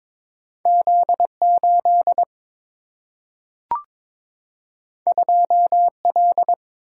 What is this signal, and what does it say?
Morse code, heard as a keyed tone, about 22 wpm, 700 Hz. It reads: Z8 2L
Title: Single-Letter Single-Number 22wpm